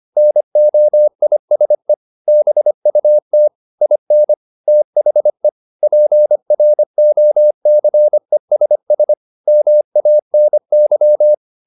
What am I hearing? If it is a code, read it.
Morse code, heard as a keyed tone, about 25 wpm, 600 Hz. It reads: NOISE BUT IN THE PROCESS MANY